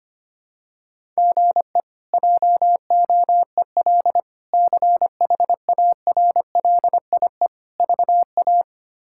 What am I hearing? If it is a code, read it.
Morse code, heard as a keyed tone, about 25 wpm, 700 Hz. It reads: GE JOEL CHARLIE VA
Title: K1USN SST QSO Practice - 25wpm